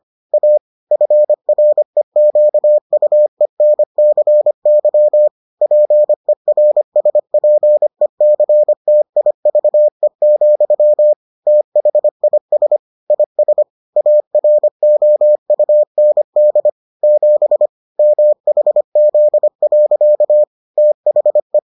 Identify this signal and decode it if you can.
Morse code, heard as a keyed tone, about 25 wpm, 600 Hz. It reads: A FREQUENCY PERSPECTIVE, THIS IS AROUND 7 MHZ. THE